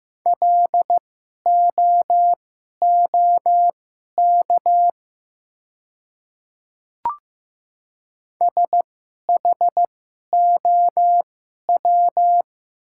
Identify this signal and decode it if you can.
Morse code, heard as a keyed tone, about 15 wpm, spaced as for 11 wpm, 690 Hz. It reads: LOOK SHOW
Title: 2x Character Spacing - Top 500 Words - 15wpm